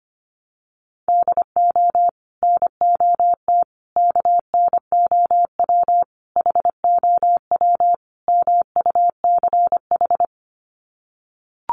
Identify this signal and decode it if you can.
Morse code, heard as a keyed tone, about 25 wpm, 700 Hz. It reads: DO NOT KNOW HOW MUCH
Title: Sets of 5 Words Encore 25wpm